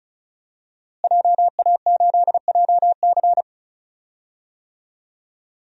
Morse code, heard as a keyed tone, about 34 wpm, 700 Hz. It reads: JA8JC
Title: DXCC-100 Call Signs - 5 Characters - Repeated 2x - 35wpm